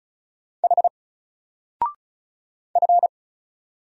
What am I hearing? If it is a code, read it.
Morse code, heard as a keyed tone, about 35 wpm, 700 Hz. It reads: H F
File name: Single-Letter-Number-35wpm